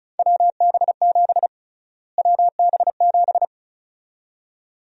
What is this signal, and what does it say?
Morse code, heard as a keyed tone, about 34 wpm, 700 Hz. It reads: WB7 WB7
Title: USA Call Sign Prefixes - Repeated 2x - 35wpm